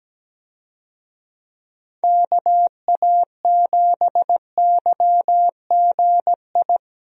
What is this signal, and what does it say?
Morse code, heard as a keyed tone, about 17 wpm, 700 Hz. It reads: KA7YGI
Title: USA Call Signs - 6 Characters - Repeated 3x - 17wpm